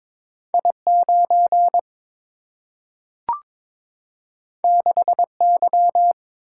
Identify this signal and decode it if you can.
Morse code, heard as a keyed tone, about 22 wpm, 700 Hz. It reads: I 9 6 Y
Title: Single-Letter Single-Number 22wpm